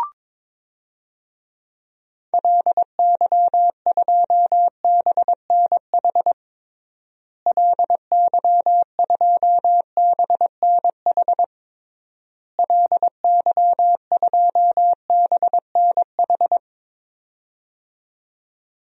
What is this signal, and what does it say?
Morse code, heard as a keyed tone, about 22 wpm, 700 Hz. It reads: LY2BNH LY2BNH LY2BNH
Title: DXCC-100 Call Signs - 6 Characters - Repeated 3x - 22wpm